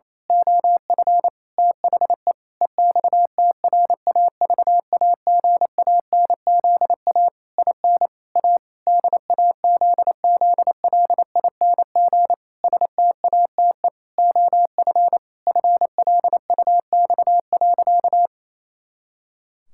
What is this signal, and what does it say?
Morse code, heard as a keyed tone, about 28 wpm, 700 Hz. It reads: OF THE EXTRAVAGANZA IN A DAZZLING STATE OF FLUX.
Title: Pangram Challenge v2 - 28wpm